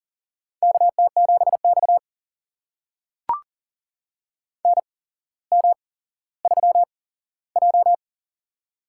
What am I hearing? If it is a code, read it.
Morse code, heard as a keyed tone, about 40 wpm, spaced as for 17 wpm, 700 Hz. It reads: KT7X NM3J
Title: USA Call Signs - 4 Characters - 8x Character Spacing - 40wpm